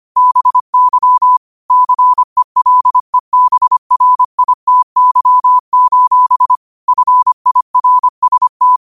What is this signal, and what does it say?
Morse code, heard as a keyed tone, about 25 wpm, 1000 Hz. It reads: BY CELEBRITY: FIRST
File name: qotdfiles / 25wpm